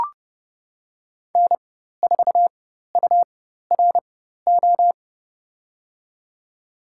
Morse code, heard as a keyed tone, about 30 wpm, spaced as for 14 wpm, 700 Hz. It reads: N4URO
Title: USA Call Signs - 5 Characters - 4x Character Spacing - 30wpm